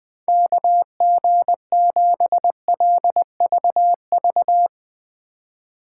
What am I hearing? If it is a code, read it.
Morse code, heard as a keyed tone, about 20 wpm, 700 Hz. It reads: KG7LVV
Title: Speed Racing - USA 2x3 Call Signs 20wpm